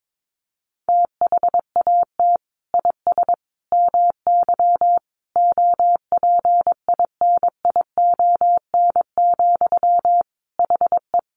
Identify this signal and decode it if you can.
Morse code, heard as a keyed tone, about 22 wpm, 700 Hz. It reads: THAT IS MY OPINION, HE
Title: Sentences from Top 1200 Words 22wpm